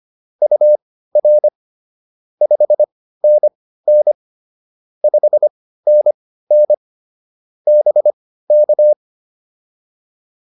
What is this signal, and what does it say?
Morse code, heard as a keyed tone, about 25 wpm, spaced as for 15 wpm, 600 Hz. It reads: UR 5NN 5NN BK
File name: POTA105 15